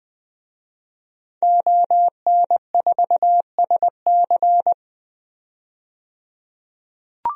Morse code, heard as a keyed tone, about 20 wpm, 700 Hz. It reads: ON4SC
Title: DXCC-100 Call Signs - 5 Characters - 20wpm